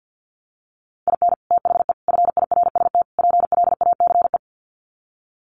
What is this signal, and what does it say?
Morse code, heard as a keyed tone, about 50 wpm, 700 Hz. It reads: IN THE FIRST PLACE
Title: Sets of 4 Words 50wpm